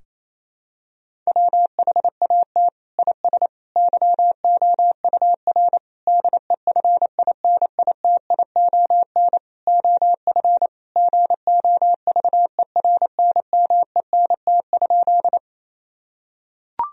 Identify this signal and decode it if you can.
Morse code, heard as a keyed tone, about 28 wpm, 700 Hz. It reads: WHAT IS YOUR DEFINITION OF GOVERNMENT?
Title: Sentences from Top 1000 Words - Review Part 4 - 28wpm